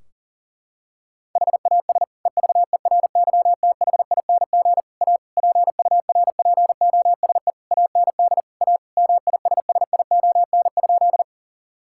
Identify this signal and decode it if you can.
Morse code, heard as a keyed tone, about 37 wpm, 700 Hz. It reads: HAS EVERYTHING A PURPOSE AND A MISSION?
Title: Sentences from Top 1600 Words 40wpm